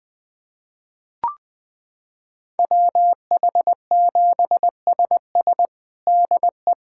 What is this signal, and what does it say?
Morse code, heard as a keyed tone, about 20 wpm, 700 Hz. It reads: WH7SS DE
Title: USA Callsign DE Callsign - 5 Characters - 20wpm